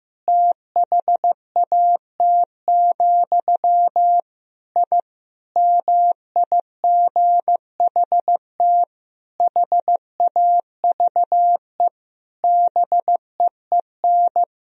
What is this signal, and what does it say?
Morse code, heard as a keyed tone, about 15 wpm, 700 Hz. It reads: THAT, I MIGHT HAVE BEEN
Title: Sentences from Top 1000 Words - Review Part 3 - 15wpm